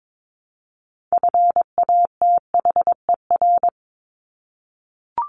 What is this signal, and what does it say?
Morse code, heard as a keyed tone, about 22 wpm, 700 Hz. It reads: FATHER
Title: Top 200 Words Repeated 22wpm